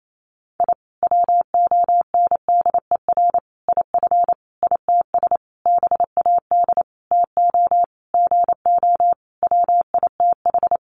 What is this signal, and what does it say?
Morse code, heard as a keyed tone, about 28 wpm, 700 Hz. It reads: I WONDER IF ITS BAD TO GO WITH
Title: Sentences from Top 700 Words 28wpm